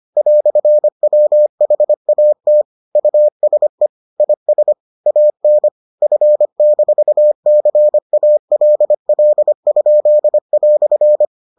Morse code, heard as a keyed tone, about 25 wpm, 600 Hz. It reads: "WHAT USE IS AN F-CALL?"